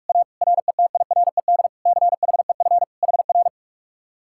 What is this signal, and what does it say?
Morse code, heard as a keyed tone, about 49 wpm, 700 Hz. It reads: A RETIRED CHEF HR